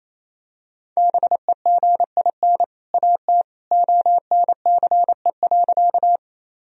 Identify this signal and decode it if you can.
Morse code, heard as a keyed tone, about 28 wpm, 700 Hz. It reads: BEGIN AT ONCE.
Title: Sentences from Top 1000 Words - Review Part 1 - 28wpm